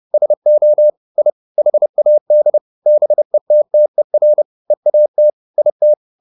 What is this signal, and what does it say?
Morse code, heard as a keyed tone, about 30 wpm, 600 Hz. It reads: SO I HAD BETTER EAT IT
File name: the old man and the sea / the old man and the sea 1 30wpm